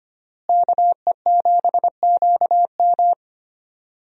Morse code, heard as a keyed tone, about 25 wpm, 700 Hz. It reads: KE7QM
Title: USA Call Signs - 5 Characters - Repeated 2x - 25wpm